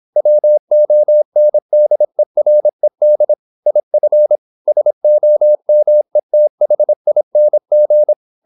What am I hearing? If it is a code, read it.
Morse code, heard as a keyed tone, about 26 wpm, 600 Hz. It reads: WONDERED IF SOMETHING
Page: Never 26wpm | CW med Gnister